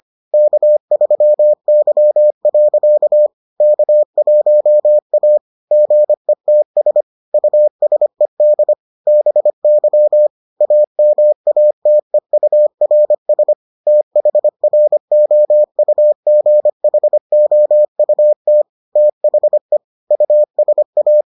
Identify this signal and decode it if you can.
Morse code, heard as a keyed tone, about 25 wpm, 600 Hz. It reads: K3Y. K1A GETS USED BY AMATEURS THROUGHOUT THE USA